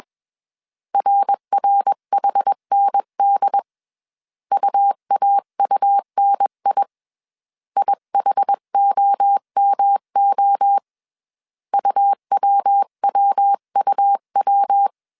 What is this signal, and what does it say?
Morse code, heard as a keyed tone, about 21 wpm, 800 Hz. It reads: RRHND UAUNI IHOMO UWWUW
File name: 20wpm